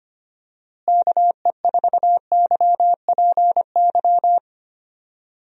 Morse code, heard as a keyed tone, about 25 wpm, 700 Hz. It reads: KE4YPY
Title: USA Call Signs - 6 Characters - Repeated 3x - 25wpm